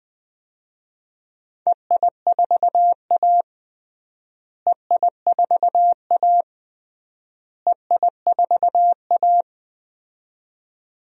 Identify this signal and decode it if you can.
Morse code, heard as a keyed tone, about 20 wpm, 700 Hz. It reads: EI4A EI4A EI4A
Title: DXCC-100 Call Signs - 4 Characters - Repeated 3x - 20wpm